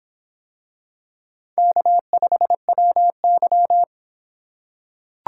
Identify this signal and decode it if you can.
Morse code, heard as a keyed tone, about 25 wpm, 700 Hz. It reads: K5WY